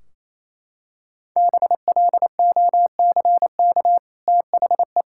Morse code, heard as a keyed tone, about 28 wpm, 700 Hz. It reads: BLOCK THE
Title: Sentences from Top 1600 Words 28wpm